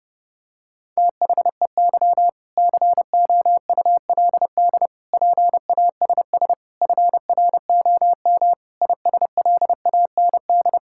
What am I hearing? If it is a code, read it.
Morse code, heard as a keyed tone, about 32 wpm, 700 Hz. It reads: THEY COULD PASS FROM ISLAND
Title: Sentences from Top 1000 Words - Review Part 6 - 30wpm